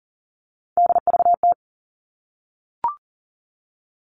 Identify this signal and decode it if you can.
Morse code, heard as a keyed tone, about 40 wpm, 700 Hz. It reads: DVT
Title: All Cut Numbers - 3-Digit 40wpm